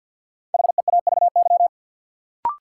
Morse code, heard as a keyed tone, about 46 wpm, 700 Hz. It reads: HEAVY
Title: Sentences from Top 1100 Words - New Word Drill Alt - 50wpm